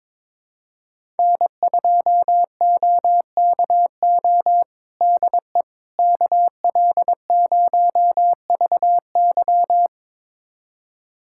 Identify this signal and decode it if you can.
Morse code, heard as a keyed tone, about 22 wpm, 700 Hz. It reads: N2OKO DE KL0VY
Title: USA Callsign DE Callsign 22wpm